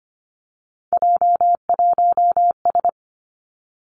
Morse code, heard as a keyed tone, about 25 wpm, 700 Hz. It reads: J1S
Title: Three Challenging Letter-Number Combinations 25wpm